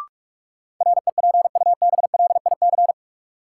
Morse code, heard as a keyed tone, about 46 wpm, 700 Hz. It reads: REPUBLIC
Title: Sentences from Top 1500 Words - New Word Drill Alt - 45wpm